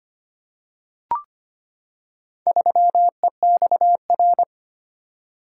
Morse code, heard as a keyed tone, about 25 wpm, 700 Hz. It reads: 3EXR
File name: practice / 25wpm / Four Letter-Number Combinations - 25wpm